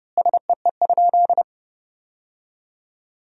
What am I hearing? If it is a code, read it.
Morse code, heard as a keyed tone, about 32 wpm, 700 Hz. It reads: SEE?
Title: 4x Word-Spacing - Sentences from Top 100 Words - 30wpm